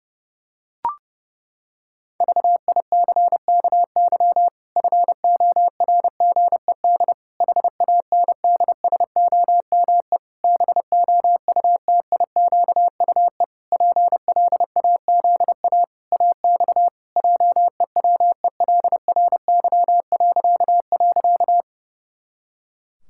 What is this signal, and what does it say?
Morse code, heard as a keyed tone, about 30 wpm, 700 Hz. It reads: VICKY FORGED HANDSOME BOUTIQUE PLAZA AX JEWELRY..
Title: Pangram Challenge v2 - 30wpm